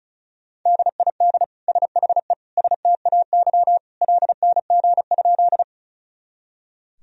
Morse code, heard as a keyed tone, about 34 wpm, 700 Hz. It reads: DID SHE STAY LNG?
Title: Sentences from Top 500 Words and Abbreviations 35wpm